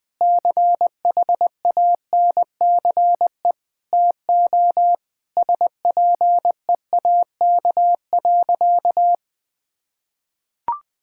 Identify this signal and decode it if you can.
Morse code, heard as a keyed tone, about 20 wpm, 700 Hz. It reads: CHANCE TO SPEAK.
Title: Sentences with 5-Word Phrases - 20wpm